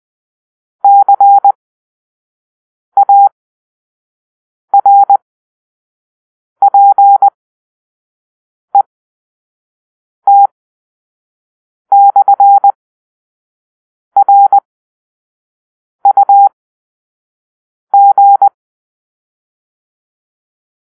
Morse code, heard as a keyed tone, about 20 wpm, spaced as for 5 wpm, 800 Hz. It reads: CARPET/RUG
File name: arrl / Grp-4 Drill-4